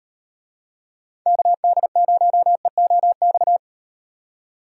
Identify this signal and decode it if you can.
Morse code, heard as a keyed tone, about 38 wpm, 700 Hz. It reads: KD0EOX